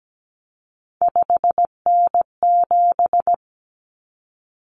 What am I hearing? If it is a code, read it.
Morse code, heard as a keyed tone, about 17 wpm, 700 Hz. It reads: 5N7
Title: Common Cut Numbers - 3-Digit 17wpm